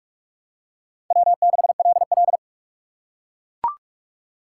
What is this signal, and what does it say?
Morse code, heard as a keyed tone, about 42 wpm, 700 Hz. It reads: W6LL
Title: USA Call Signs - 4 Characters - 4x Character Spacing - 45wpm